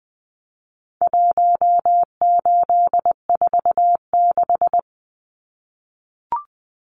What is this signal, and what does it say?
Morse code, heard as a keyed tone, about 20 wpm, 700 Hz. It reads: 1846
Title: Numbers - Contest Style - 4-Digit 20wpm